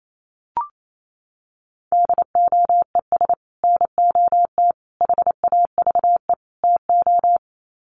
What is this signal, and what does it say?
Morse code, heard as a keyed tone, about 28 wpm, 700 Hz. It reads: DOES NOT HAVE TO
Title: Sets of 4 Words 28wpm